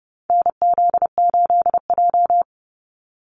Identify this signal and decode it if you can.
Morse code, heard as a keyed tone, about 30 wpm, 700 Hz. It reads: NZ8J
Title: Call Signs - 30wpm